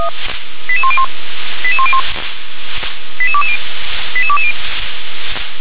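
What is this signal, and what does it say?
Five-tone selective call (ZVEI), heard as quick runs of five tones, tones of about 70 ms. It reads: 88101 88101 88220 88220
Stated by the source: ZVEI five-tone selective call 88126, received on an SDR.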